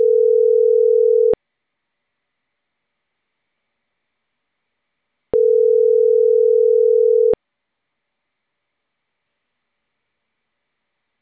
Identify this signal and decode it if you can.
Ringback tone (North America), 440 + 480 Hz.